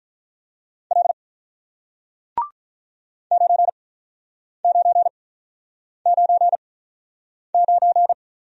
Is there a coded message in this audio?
Yes: Morse code from 0.9 to 8.1 s, heard as a keyed tone, about 48 wpm, spaced as for 9 wpm, 700 Hz. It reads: L 9999